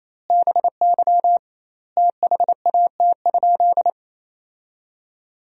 Morse code, heard as a keyed tone, about 28 wpm, 700 Hz. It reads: BY THAT?
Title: 2x Word-Spacing - Sentences from Top 200 Words - 28wpm